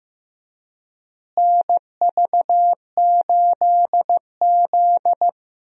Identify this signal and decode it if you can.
Morse code, heard as a keyed tone, about 15 wpm, 700 Hz. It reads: NV8Z
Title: USA Call Signs - 4 Characters - Repeated 3x - 15wpm